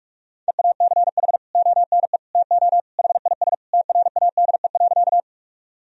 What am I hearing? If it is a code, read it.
Morse code, heard as a keyed tone, about 41 wpm, 700 Hz. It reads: EACH ONE TO HIS TRADE.
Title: Sentences from Top 1000 Words - Review Part 4 - 45wpm